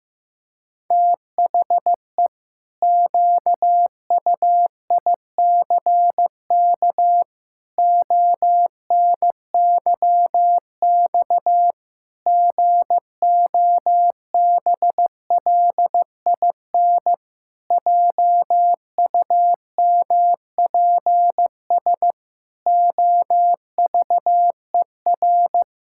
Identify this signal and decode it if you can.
Morse code, heard as a keyed tone, about 15 wpm, 700 Hz. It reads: THE QUICK ONYX GOBLIN JUMPS OVER
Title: Pangram Challenge v2 - 15wpm